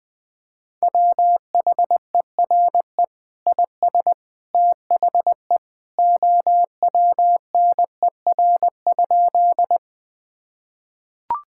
Morse code, heard as a keyed tone, about 19 wpm, 700 Hz. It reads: WHERE IS THE OWNER?
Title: Sentences from Top 1600 Words 20wpm